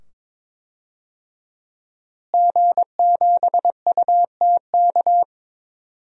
Morse code, heard as a keyed tone, about 21 wpm, 700 Hz. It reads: G7UTK
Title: DXCC-100 Call Signs - 5 Characters - Repeated 2x - 22wpm